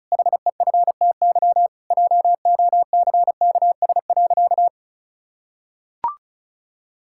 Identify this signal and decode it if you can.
Morse code, heard as a keyed tone, about 36 wpm, 700 Hz. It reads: HEFTY JOCKS.
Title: Pangram Challenge v2 - 35wpm